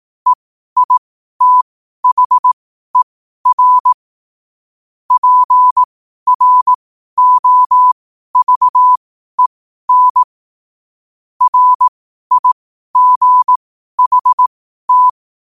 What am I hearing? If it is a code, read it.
Morse code, heard as a keyed tone, about 18 wpm, spaced as for 12 wpm, 1000 Hz. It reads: EITHER PROVEN RIGHT